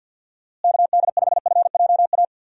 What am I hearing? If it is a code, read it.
Morse code, heard as a keyed tone, about 52 wpm, 700 Hz. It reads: KD5FJA
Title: USA Call Signs - 6 Characters - Repeated 2x - 50wpm